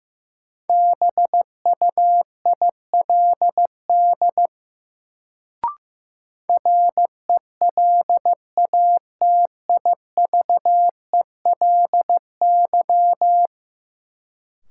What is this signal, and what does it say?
Morse code, heard as a keyed tone, about 15 wpm, 700 Hz. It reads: BUILD RELATIVELY
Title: Sentences from Top 1100 Words - New Word Drill Alt - 15wpm